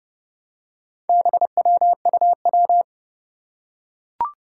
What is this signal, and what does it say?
Morse code, heard as a keyed tone, about 30 wpm, 700 Hz. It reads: BWUW
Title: Four Challenging Letter-Number Combinations 30wpm